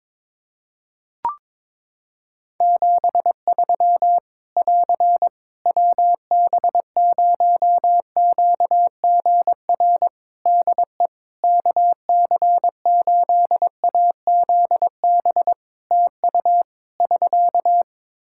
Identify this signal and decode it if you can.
Morse code, heard as a keyed tone, about 22 wpm, 700 Hz. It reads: 73 <AR> WB0QGR DE KC8AZB TU <SK>
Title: Practice QSO - Exchange 4 - Ending - 22wpm